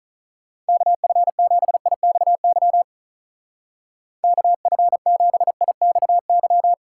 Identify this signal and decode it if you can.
Morse code, heard as a keyed tone, about 42 wpm, 700 Hz. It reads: KF7IXY KF7IXY
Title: Speed Racing - USA Call Signs 35wpm